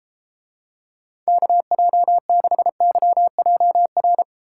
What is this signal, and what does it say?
Morse code, heard as a keyed tone, about 30 wpm, 700 Hz. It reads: KJ6YJR